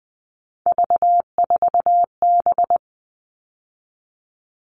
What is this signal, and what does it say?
Morse code, heard as a keyed tone, about 20 wpm, 700 Hz. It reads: V4B
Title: All Cut Numbers - 3-Digit 20wpm